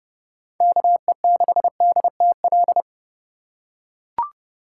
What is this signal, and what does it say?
Morse code, heard as a keyed tone, about 30 wpm, 700 Hz. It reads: KE6DTL
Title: USA Call Signs - 6 Characters - 2x Character Spacing - 30wpm